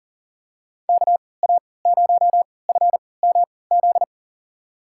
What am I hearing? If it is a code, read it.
Morse code, heard as a keyed tone, about 40 wpm, spaced as for 23 wpm, 700 Hz. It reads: KA0FMZ
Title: USA Call Signs - 6 Characters - 3x Character Spacing - 40wpm